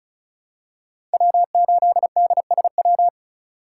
Morse code, heard as a keyed tone, about 36 wpm, 700 Hz. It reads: W8DSW
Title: USA Call Signs - 5 Characters - 2x Character Spacing - 35wpm